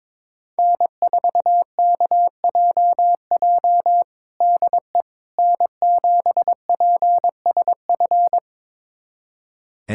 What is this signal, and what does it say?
Morse code, heard as a keyed tone, about 22 wpm, 700 Hz. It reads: N4KJJ DE N7PSF
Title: USA Callsign DE Callsign - 5 Characters - 22wpm